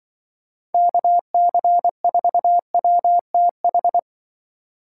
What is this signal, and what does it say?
Morse code, heard as a keyed tone, about 25 wpm, 700 Hz. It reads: KC4WTH
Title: Speed Racing - USA 2x3 Call Signs 20wpm